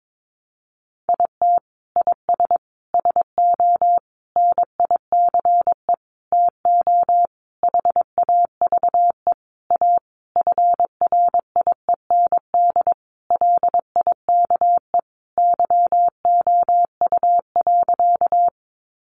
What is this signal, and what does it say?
Morse code, heard as a keyed tone, about 22 wpm, 700 Hz. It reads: IT IS SO NICE TO HAVE A FRIEND LIKE YOU.
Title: Sentences from Top 1100 Words 22wpm